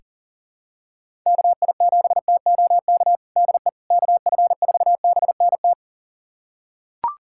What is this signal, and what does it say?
Morse code, heard as a keyed tone, about 40 wpm, 700 Hz. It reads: KI7TOK DE KF4BNT
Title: USA Callsign DE Callsign - 6 Characters - 40wpm